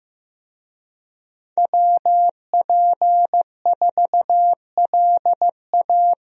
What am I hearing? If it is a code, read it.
Morse code, heard as a keyed tone, about 15 wpm, 700 Hz. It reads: WP4LA
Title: USA Call Signs - 5 Characters - Repeated 3x - 15wpm